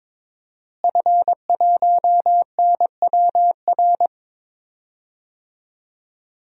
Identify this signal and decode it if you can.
Morse code, heard as a keyed tone, about 21 wpm, 700 Hz. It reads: F1NWR
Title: DXCC-100 Call Signs - 5 Characters - Repeated 2x - 22wpm